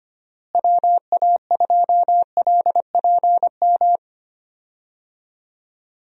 Morse code, heard as a keyed tone, about 25 wpm, 700 Hz. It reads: WA2LPM
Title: USA Call Signs - 6 Characters - Repeated 3x - 25wpm